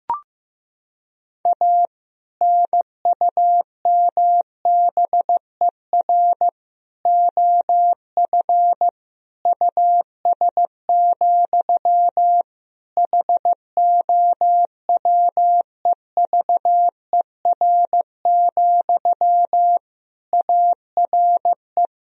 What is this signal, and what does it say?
Morse code, heard as a keyed tone, about 15 wpm, 700 Hz. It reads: A NUMBER OF US, HOWEVER, ARE
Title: Sentences from Top 1000 Words - Review Part 1 - 15wpm